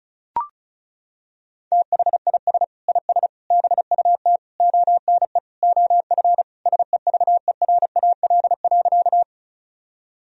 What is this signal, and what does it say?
Morse code, heard as a keyed tone, about 34 wpm, 700 Hz. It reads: THIS IS BUT ONE OF SEVERAL.
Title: Sentences from Top 1000 Words - Review Part 2 - 35wpm